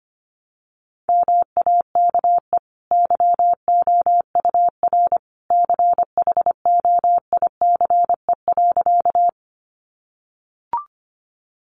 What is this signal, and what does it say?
Morse code, heard as a keyed tone, about 25 wpm, 700 Hz. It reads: MAKE YOUR CHOICE.
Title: Sentences from Top 900 Words 25wpm